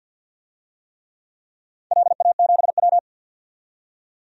Morse code, heard as a keyed tone, about 50 wpm, 700 Hz. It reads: LA7W
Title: DXCC-100 Call Signs - 4 Characters - 50wpm